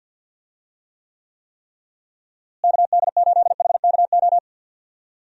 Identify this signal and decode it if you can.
Morse code, heard as a keyed tone, about 50 wpm, 700 Hz. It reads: KD8HKO
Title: USA Call Signs - 6 Characters - Repeated 3x - 50wpm